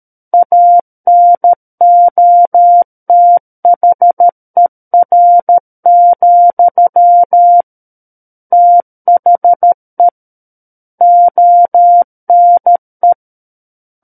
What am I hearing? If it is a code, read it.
Morse code, heard as a keyed tone, about 13 wpm, 700 Hz. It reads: ANOTHER, THE ONE
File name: Book One / 04-Chapter 4 - 13 WPM